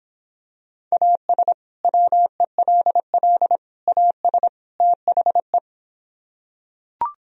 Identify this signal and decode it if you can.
Morse code, heard as a keyed tone, about 26 wpm, 700 Hz. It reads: AS WELL AS THE